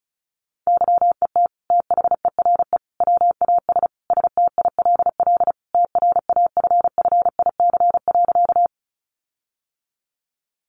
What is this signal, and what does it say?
Morse code, heard as a keyed tone, about 34 wpm, 700 Hz. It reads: YET THERE WAS STILL TRAFFIC.
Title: Sentences from Top 1300 Words 35wpm